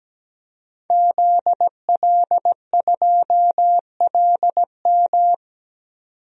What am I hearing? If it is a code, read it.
Morse code, heard as a keyed tone, about 17 wpm, 700 Hz. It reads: ZL2LM
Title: DXCC-100 Call Signs - 5 Characters - 17wpm